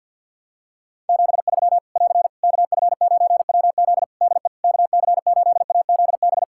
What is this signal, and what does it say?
Morse code, heard as a keyed tone, about 51 wpm, 700 Hz. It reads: 73 <AR> KF9WZ DE KC8AZB
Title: Practice QSO - Exchange 4 - Ending - 50wpm